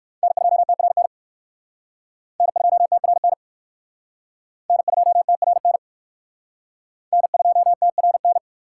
Morse code, heard as a keyed tone, about 56 wpm, 700 Hz. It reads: N2TRN N2TRN N2TRN N2TRN
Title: Speed Racing - USA 1x3 Call Signs 45wpm